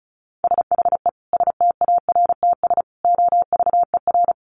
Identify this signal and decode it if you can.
Morse code, heard as a keyed tone, about 34 wpm, 700 Hz. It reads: SHE STARTS OVER
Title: Easy Phrases 35wpm